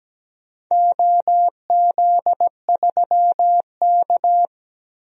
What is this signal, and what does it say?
Morse code, heard as a keyed tone, about 17 wpm, 700 Hz. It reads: OZ3K
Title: DXCC-100 Call Signs - 4 Characters - Repeated 2x - 17wpm